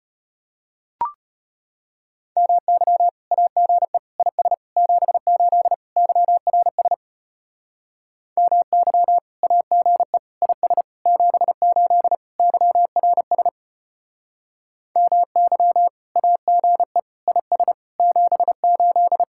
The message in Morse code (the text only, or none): MY AGE IS 78 YRS MY AGE IS 78 YRS MY AGE IS 78